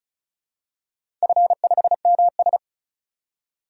Morse code, heard as a keyed tone, about 36 wpm, 690 Hz. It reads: F5MS
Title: DXCC-100 Call Signs - 4 Characters - 35wpm